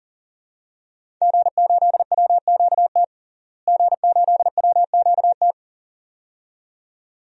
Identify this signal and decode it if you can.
Morse code, heard as a keyed tone, about 40 wpm, 700 Hz. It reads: G8WQT G8WQT
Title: DXCC-100 Call Signs - 5 Characters - Repeated 2x - 40wpm